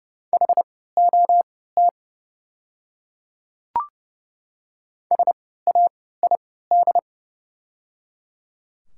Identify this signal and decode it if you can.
Morse code, heard as a keyed tone, about 30 wpm, spaced as for 17 wpm, 700 Hz. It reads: HOT SAID
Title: 3x Character Spacing - Top 100 Words - 30wpm